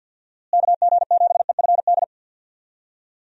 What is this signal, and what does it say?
Morse code, heard as a keyed tone, about 52 wpm, 700 Hz. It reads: KG7EFD
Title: USA Call Signs - 6 Characters - Repeated 3x - 50wpm